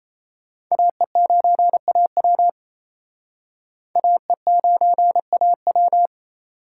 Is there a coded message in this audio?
Yes: Morse code from 0.7 to 6.1 s, heard as a keyed tone, about 37 wpm, 700 Hz. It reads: AE9AW AE9AW